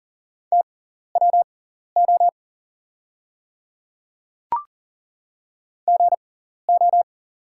Morse code, heard as a keyed tone, about 40 wpm, spaced as for 14 wpm, 690 Hz. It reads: TWO GO
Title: 6x Character Spacing - Top 100 Words - 40wpm